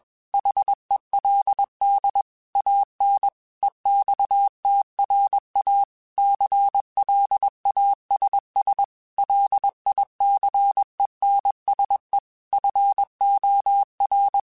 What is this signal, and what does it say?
Morse code, heard as a keyed tone, about 21 wpm, 800 Hz. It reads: HELD AN EXTRA CLASS LICENSE FOR
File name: QSOs / QSOs-mp3s / WA7QOX 2@20